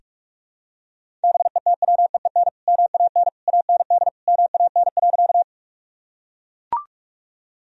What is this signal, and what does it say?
Morse code, heard as a keyed tone, about 42 wpm, 700 Hz. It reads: BETWEEN MAN AND MAN.
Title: Sentences from Top 1000 Words - Review Part 1 - 45wpm